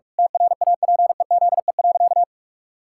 Morse code, heard as a keyed tone, about 43 wpm, 700 Hz. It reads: TRAPEZE.
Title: Pangram Challenge v2 - 45wpm